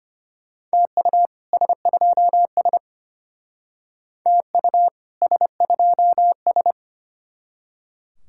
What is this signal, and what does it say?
Morse code, heard as a keyed tone, about 32 wpm, 700 Hz. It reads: TU S2S TU S2S